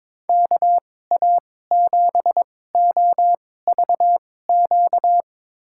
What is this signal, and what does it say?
Morse code, heard as a keyed tone, about 23 wpm, spaced as for 16 wpm, 700 Hz. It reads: KA7OVQ
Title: USA Call Signs - 6 Characters - 2x Character Spacing - 22wpm